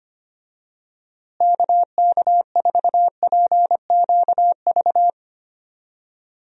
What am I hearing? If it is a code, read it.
Morse code, heard as a keyed tone, about 25 wpm, 700 Hz. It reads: KK4PQV
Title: USA Call Signs - 6 Characters - Repeated 2x - 25wpm